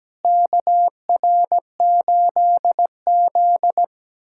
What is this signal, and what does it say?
Morse code, heard as a keyed tone, about 17 wpm, 700 Hz. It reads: KR8Z
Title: USA Callsign DE Callsign - 4 Characters - 17wpm